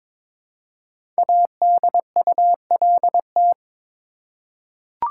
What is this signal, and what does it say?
Morse code, heard as a keyed tone, about 22 wpm, 700 Hz. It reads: ADULT
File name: 5 Letter Words - First 3 Letters Given - 22wpm